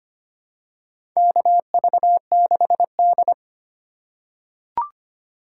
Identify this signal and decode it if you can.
Morse code, heard as a keyed tone, about 25 wpm, 700 Hz. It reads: KV6D
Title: USA Call Signs - 4 Characters - 3x Character Spacing - 25wpm